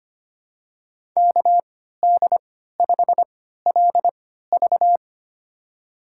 Morse code, heard as a keyed tone, about 25 wpm, spaced as for 14 wpm, 700 Hz. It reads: KD5LV
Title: USA Call Signs - 5 Characters - 3x Character Spacing - 25wpm